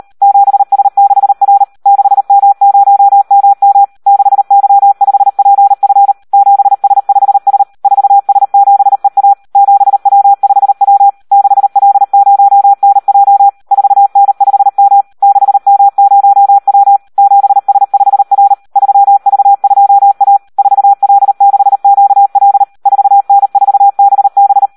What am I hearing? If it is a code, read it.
Morse code, heard as a keyed tone, about 38 wpm, 800 Hz. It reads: 7S6R 6M0MM 6Y5PF 7S5S 4S7EA 7W5W 6L0NJ 4N5M 6M0W 7S5R 3V2A 4L6QL 4N4BB
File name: Audio practica CW / SD 1 38